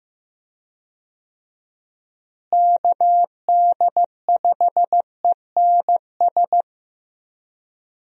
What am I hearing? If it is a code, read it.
Morse code, heard as a keyed tone, about 15 wpm, 700 Hz. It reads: KD5ENS
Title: USA Call Signs - 6 Characters - Repeated 2x - 15wpm